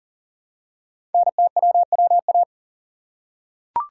Morse code, heard as a keyed tone, about 40 wpm, 700 Hz. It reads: NTWWA
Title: Code Groups - 5 Letters - 40wpm